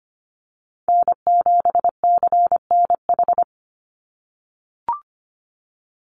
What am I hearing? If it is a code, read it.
Morse code, heard as a keyed tone, about 25 wpm, 700 Hz. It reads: N7CNH
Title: NA SOTA Chaser Callsigns 25wpm